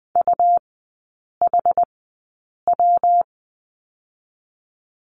Morse code, heard as a keyed tone, about 20 wpm, spaced as for 8 wpm, 700 Hz. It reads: UHW
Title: 114 Lesson 19e - Letter w - Three Characters 20wpm